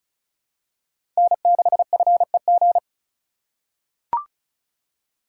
Morse code, heard as a keyed tone, about 36 wpm, 700 Hz. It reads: N6FEG
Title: USA Call Signs - 5 Characters - 2x Character Spacing - 35wpm